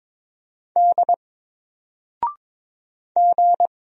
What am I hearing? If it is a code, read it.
Morse code, heard as a keyed tone, about 22 wpm, 700 Hz. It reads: D G
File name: Single-Letter-Number-22wpm